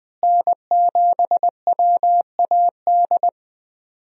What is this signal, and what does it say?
Morse code, heard as a keyed tone, about 21 wpm, 700 Hz. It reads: N7WAD
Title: USA Call Signs - 5 Characters - Repeated 2x - 20wpm